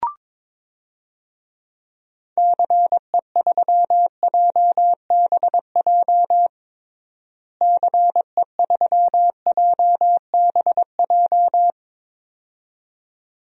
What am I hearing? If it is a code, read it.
Morse code, heard as a keyed tone, about 22 wpm, 700 Hz. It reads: CE3JBJ CE3JBJ
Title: DXCC-100 Call Signs - 6 Characters - Repeated 2x - 22wpm